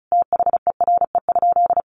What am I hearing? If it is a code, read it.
Morse code, heard as a keyed tone, about 34 wpm, 700 Hz. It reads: THERE?
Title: Sentences from Top 1200 Words 35wpm